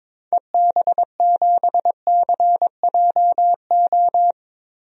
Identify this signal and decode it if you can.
Morse code, heard as a keyed tone, about 22 wpm, 700 Hz. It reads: EB7CJO
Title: DXCC-100 Call Signs - 6 Characters - Repeated 3x - 22wpm